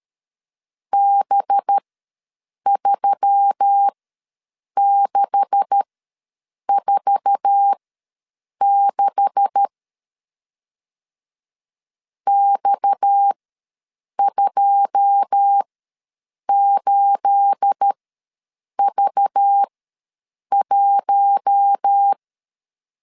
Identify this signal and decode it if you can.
Morse code, heard as a keyed tone, about 13 wpm, spaced as for 7 wpm, 800 Hz. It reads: B3646 X28V1
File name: LEZIONI / 9 286 ws 12wpm 3351